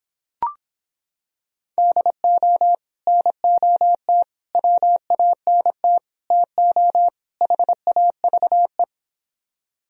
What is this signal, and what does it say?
Morse code, heard as a keyed tone, about 26 wpm, 700 Hz. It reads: DO NOT WANT TO HAVE